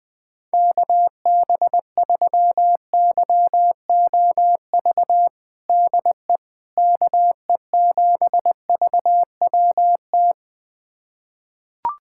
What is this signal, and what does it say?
Morse code, heard as a keyed tone, about 20 wpm, 700 Hz. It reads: KB3YOV DE KE7VWT
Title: USA Callsign DE Callsign - 6 Characters - 20wpm